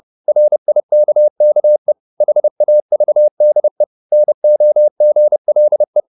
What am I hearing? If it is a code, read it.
Morse code, heard as a keyed tone, about 30 wpm, 600 Hz. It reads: RIKKE HAVDE NOGLE
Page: Ferie 30 wpm